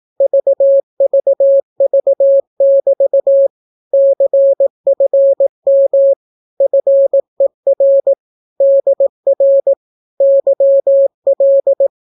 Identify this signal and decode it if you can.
Morse code, heard as a keyed tone, about 18 wpm, 550 Hz. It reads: VVV<BT> CFM FER DR YL
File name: naukapodstaw / skroty / skroty